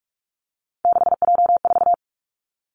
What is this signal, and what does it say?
Morse code, heard as a keyed tone, about 46 wpm, 700 Hz. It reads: 6J4
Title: Three Challenging Letter-Number Combinations 45wpm